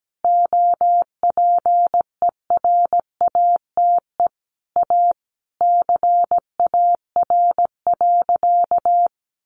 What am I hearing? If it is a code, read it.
Morse code, heard as a keyed tone, about 17 wpm, 700 Hz. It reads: OPERATE A CAR.
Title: Sentences from Top 1500 Words 17wpm